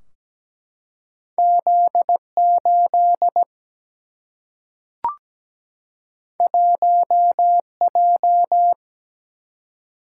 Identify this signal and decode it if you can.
Morse code, heard as a keyed tone, about 17 wpm, 700 Hz. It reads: Z8 1J
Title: Single-Letter Single-Number 17wpm